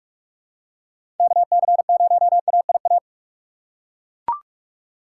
Morse code, heard as a keyed tone, about 41 wpm, 700 Hz. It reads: K C 0 A I A
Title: USA Call Signs - 6 Characters - 6x Character Spacing - 45wpm